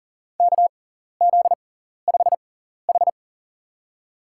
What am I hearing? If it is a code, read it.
Morse code, heard as a keyed tone, about 40 wpm, spaced as for 14 wpm, 700 Hz. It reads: KZ5H
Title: USA Call Signs - 4 Characters - 6x Character Spacing - 40wpm